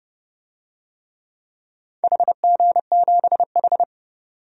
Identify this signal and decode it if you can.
Morse code, heard as a keyed tone, about 30 wpm, 700 Hz. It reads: HG7H
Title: DXCC-100 Call Signs - 4 Characters - 30wpm